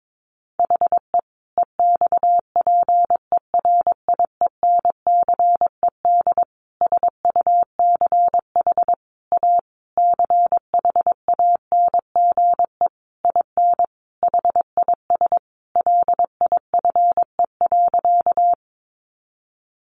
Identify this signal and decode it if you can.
Morse code, heard as a keyed tone, about 22 wpm, 700 Hz. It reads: HE EXPERIENCED SUCH A CHANGE IN HIS LIFE.
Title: Sentences from Top 1100 Words 22wpm